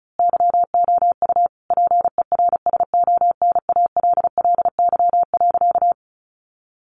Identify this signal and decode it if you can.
Morse code, heard as a keyed tone, about 34 wpm, 700 Hz. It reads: YOU PERSONALLY.
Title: Sentences from Top 1400 Words 35wpm